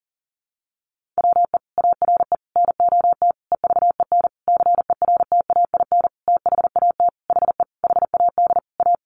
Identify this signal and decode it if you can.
Morse code, heard as a keyed tone, about 40 wpm, 700 Hz. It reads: WE ARE NOT EVEN CERTAIN THAT HE HAD A
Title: Sentences from Top 1500 Words 40wpm